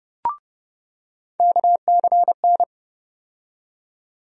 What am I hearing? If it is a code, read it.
Morse code, heard as a keyed tone, about 31 wpm, 700 Hz. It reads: KCN
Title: USA Call Sign Postfixes - 3 Letters 30wpm